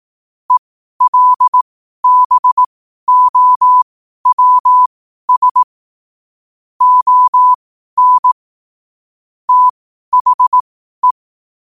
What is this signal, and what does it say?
Morse code, heard as a keyed tone, about 18 wpm, spaced as for 12 wpm, 1000 Hz. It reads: ELBOWS ON THE
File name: qotdfiles / 12wpm